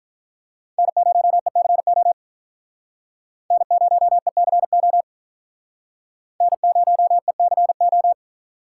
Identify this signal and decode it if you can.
Morse code, heard as a keyed tone, about 46 wpm, 700 Hz. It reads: N0ECO N0ECO N0ECO